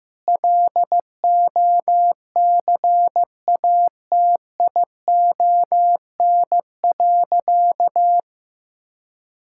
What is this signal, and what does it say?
Morse code, heard as a keyed tone, about 15 wpm, 700 Hz. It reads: LOCATION.
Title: Sentences from Top 1000 Words - Review Part 6 - 15wpm